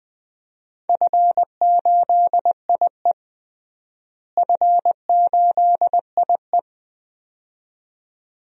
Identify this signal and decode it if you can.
Morse code, heard as a keyed tone, about 20 wpm, 700 Hz. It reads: F8IE F8IE
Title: DXCC-100 Call Signs - 4 Characters - Repeated 2x - 20wpm